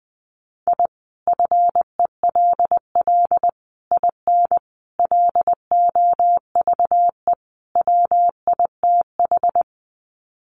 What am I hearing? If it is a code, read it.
Morse code, heard as a keyed tone, about 20 wpm, 700 Hz. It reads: I FELL IN LOVE WITH
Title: Sets of 5 Words Encore 20wpm